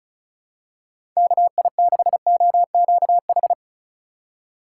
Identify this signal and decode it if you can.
Morse code, heard as a keyed tone, about 36 wpm, 700 Hz. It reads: KI6OQH
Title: USA Call Signs - 6 Characters - 2x Character Spacing - 35wpm